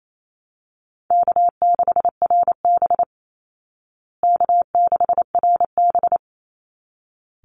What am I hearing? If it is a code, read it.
Morse code, heard as a keyed tone, about 28 wpm, 700 Hz. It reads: K6RB K6RB
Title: Easy Call Signs Repeated 2x - 28wpm